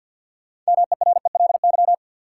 Morse code, heard as a keyed tone, about 51 wpm, 700 Hz. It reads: MERELY
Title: Sentences from Top 1400 Words - New Word Drill - 50wpm